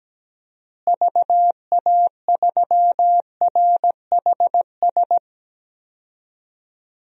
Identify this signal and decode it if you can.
Morse code, heard as a keyed tone, about 17 wpm, 700 Hz. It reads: VA3RHS
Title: DXCC-100 Call Signs - 6 Characters - Repeated 2x - 17wpm